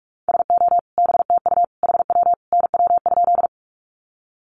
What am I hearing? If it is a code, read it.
Morse code, heard as a keyed tone, about 44 wpm, 700 Hz. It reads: SO BTU HW NW?
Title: Common QSO Phrases 45wpm